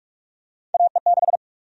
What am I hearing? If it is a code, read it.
Morse code, heard as a keyed tone, about 43 wpm, 700 Hz. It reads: AE6
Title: USA Call Sign Prefixes - 45wpm